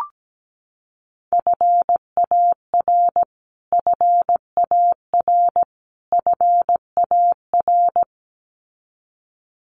Morse code, heard as a keyed tone, about 17 wpm, 700 Hz. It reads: FAR FAR FAR
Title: Top 500 Words Repeated 17wpm